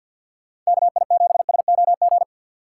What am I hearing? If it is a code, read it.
Morse code, heard as a keyed tone, about 64 wpm, spaced as for 57 wpm, 700 Hz. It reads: KI7SOG
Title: Speed Racing - USA 2x3 Call Signs 50wpm